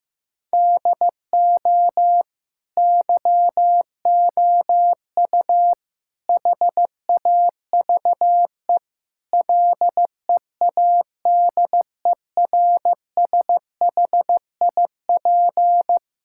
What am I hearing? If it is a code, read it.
Morse code, heard as a keyed tone, about 15 wpm, 700 Hz. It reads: DO YOU HAVE LEADERSHIP
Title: Sentences from Top 1600 Words 15wpm